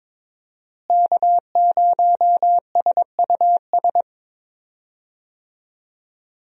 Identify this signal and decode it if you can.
Morse code, heard as a keyed tone, about 22 wpm, 700 Hz. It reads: K0SUS
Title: USA Call Signs - 5 Characters - Repeated 2x - 22wpm